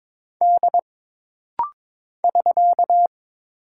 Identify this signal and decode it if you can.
Morse code, heard as a keyed tone, about 22 wpm, 700 Hz. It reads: D <SK>
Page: This rapid-fire timing is geared to push your ICR (Instant Character Recognition) skills.